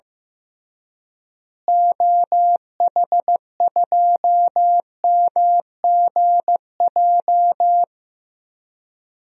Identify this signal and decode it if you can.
Morse code, heard as a keyed tone, about 15 wpm, 700 Hz. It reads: OH2MGJ
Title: DXCC-100 Call Signs - 6 Characters - Repeated 3x - 15wpm